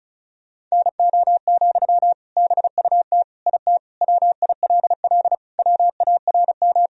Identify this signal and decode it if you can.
Morse code, heard as a keyed tone, about 34 wpm, 700 Hz. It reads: NO, BUT IT WILL WARM
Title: Sentences from Top 1000 Words - Review Part 2 - 35wpm